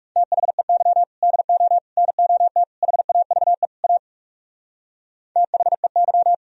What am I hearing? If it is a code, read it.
Morse code, heard as a keyed tone, about 38 wpm, 700 Hz. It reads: THEY DO NOT HAVE A THEY